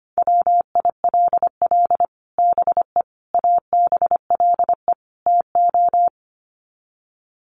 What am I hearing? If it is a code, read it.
Morse code, heard as a keyed tone, about 25 wpm, 700 Hz. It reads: WILL BE ABLE TO
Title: Sets of 4 Words 25wpm